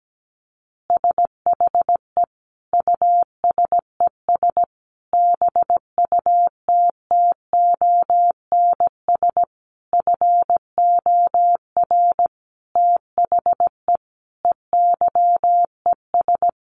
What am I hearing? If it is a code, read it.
Morse code, heard as a keyed tone, about 17 wpm, 700 Hz. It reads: SHE USES BUTTONS FOR THE EYES
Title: Easy Phrases 17wpm